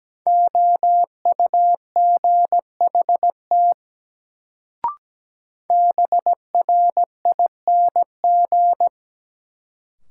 Morse code, heard as a keyed tone, about 17 wpm, 700 Hz. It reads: OUGHT BRING
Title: Sets of 5 Words Encore - New Word Drill - 17wpm